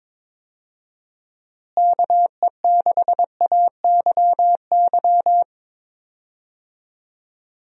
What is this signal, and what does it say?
Morse code, heard as a keyed tone, about 22 wpm, 700 Hz. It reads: KE6AYY
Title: USA Call Signs - 6 Characters - Repeated 2x - 22wpm